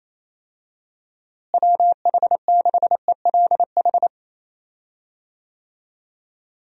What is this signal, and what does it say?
Morse code, heard as a keyed tone, about 28 wpm, 700 Hz. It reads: WH6ELH
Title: USA Call Signs - 6 Characters - Repeated 2x - 28wpm